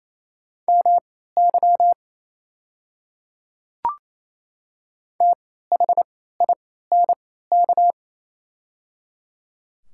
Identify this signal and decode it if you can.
Morse code, heard as a keyed tone, about 28 wpm, spaced as for 16 wpm, 700 Hz. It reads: MY THINK
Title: 3x Character Spacing - Top 300 Words - 28wpm